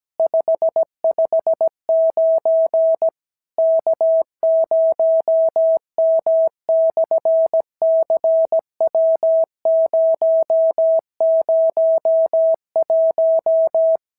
Morse code, heard as a keyed tone, about 17 wpm, 690 Hz. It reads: 559 K0M/CW001
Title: SOTA QSO Practice 17wpm